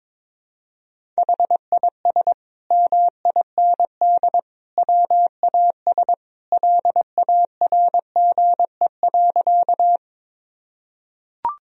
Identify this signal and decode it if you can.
Morse code, heard as a keyed tone, about 22 wpm, 700 Hz. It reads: HIS MIND WAS LARGE.
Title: Sentences from Top 1000 Words - Review Part 1 - 22wpm